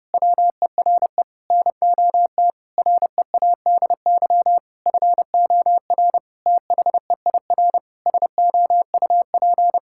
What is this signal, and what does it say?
Morse code, heard as a keyed tone, about 30 wpm, 700 Hz. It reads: WERE NOT READY FOR THEIR SOUP
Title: Sentences from Top 1000 Words - Review Part 6 - 30wpm